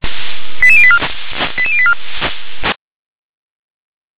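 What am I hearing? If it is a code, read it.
Five-tone selective call (ZVEI), heard as quick runs of five tones, tones of about 70 ms. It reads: 88084 88084
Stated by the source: ZVEI five-tone selective call received on SDR.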